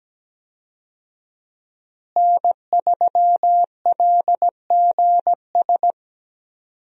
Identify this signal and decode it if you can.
Morse code, heard as a keyed tone, about 17 wpm, 700 Hz. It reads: N3LGS
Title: USA Call Signs - 5 Characters - Repeated 3x - 17wpm